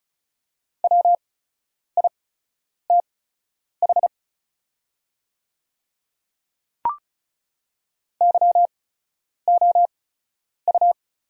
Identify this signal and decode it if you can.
Morse code, heard as a keyed tone, about 36 wpm, spaced as for 10 wpm, 700 Hz. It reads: WITH YOU
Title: 8x Character Spacing - Top 100 Words - 35wpm